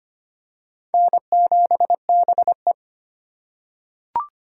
Morse code, heard as a keyed tone, about 25 wpm, 700 Hz. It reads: N7BE
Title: USA Call Signs - 4 Characters - 3x Character Spacing - 25wpm